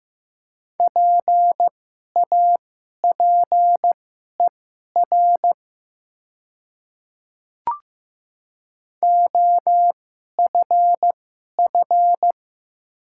Morse code, heard as a keyed tone, about 15 wpm, spaced as for 11 wpm, 700 Hz. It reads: PAPER OFF
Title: 2x Character Spacing - Top 500 Words - 15wpm